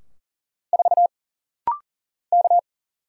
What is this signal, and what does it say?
Morse code, heard as a keyed tone, about 39 wpm, 700 Hz. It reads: 4 K
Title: Single Letter-Number Alt - Rapid-Fire 40wpm